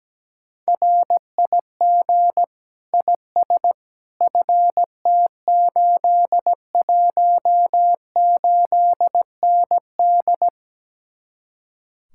Morse code, heard as a keyed tone, about 17 wpm, 700 Hz. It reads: RIG IS FT818ND
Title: Speed-Racing Common QSO Phrases - 17wpm